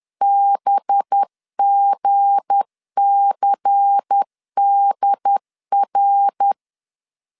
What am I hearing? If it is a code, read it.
Morse code, heard as a keyed tone, about 10 wpm, 800 Hz. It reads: BGCDR
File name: CW / DET / 1 9 1500C S A 10wpm 4511